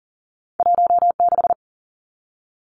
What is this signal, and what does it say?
Morse code, heard as a keyed tone, about 40 wpm, 700 Hz. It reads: 16
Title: Numbers - 2-Digit 40wpm